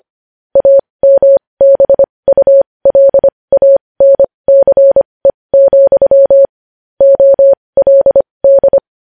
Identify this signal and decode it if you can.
Morse code, heard as a keyed tone, about 25 wpm, spaced as for 20 wpm, 550 Hz. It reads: AMBULANCE, OLD
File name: podcast / News headlines-1x-25x20